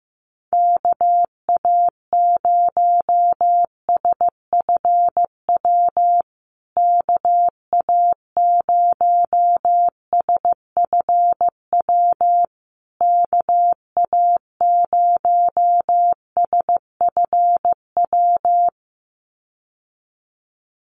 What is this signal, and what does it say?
Morse code, heard as a keyed tone, about 15 wpm, 700 Hz. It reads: KA0SFW KA0SFW KA0SFW
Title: USA Call Signs Repeated 3x 15wpm